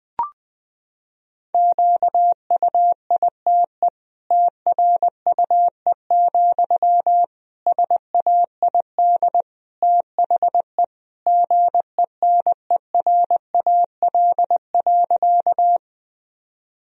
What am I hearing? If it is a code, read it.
Morse code, heard as a keyed tone, about 20 wpm, 700 Hz. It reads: QUITE TRUE, SAID THE GENERAL.
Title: Sentences from Top 1000 Words - Review Part 3 - 20wpm